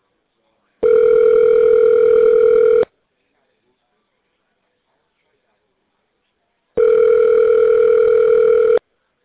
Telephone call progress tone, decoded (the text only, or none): ringback tone